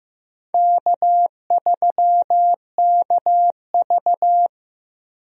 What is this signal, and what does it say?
Morse code, heard as a keyed tone, about 15 wpm, 700 Hz. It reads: K3KV
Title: USA Call Signs - 4 Characters - 15wpm